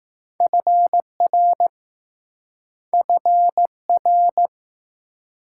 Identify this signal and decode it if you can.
Morse code, heard as a keyed tone, about 18 wpm, 700 Hz. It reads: FR FR
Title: Speed-Racing Multi-letter Phonemes - 15wpm